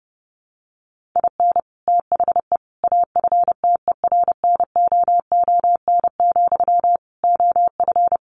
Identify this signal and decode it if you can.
Morse code, heard as a keyed tone, about 30 wpm, 700 Hz. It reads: IN THE AFTERNOON, OF
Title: Sentences from Top 1500 Words 30wpm